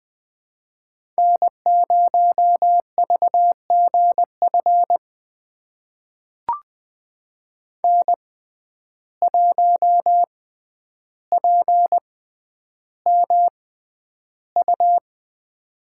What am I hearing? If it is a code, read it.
Morse code, heard as a keyed tone, about 20 wpm, spaced as for 10 wpm, 700 Hz. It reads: N0VGF N1PMU
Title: USA Call Signs - 5 Characters - 6x Character Spacing - 20wpm